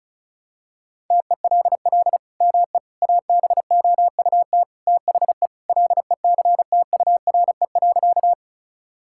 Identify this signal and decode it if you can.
Morse code, heard as a keyed tone, about 36 wpm, 700 Hz. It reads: TELL ME ABOUT THE LECTURE.
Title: Sentences from Top 1600 Words 35wpm